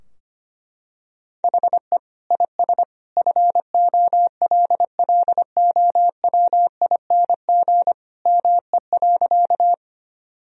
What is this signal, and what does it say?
Morse code, heard as a keyed tone, about 25 wpm, 700 Hz. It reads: HE IS FOLLOWING ME.
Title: Sentences from Top 1000 Words - Review Part 4 - 25wpm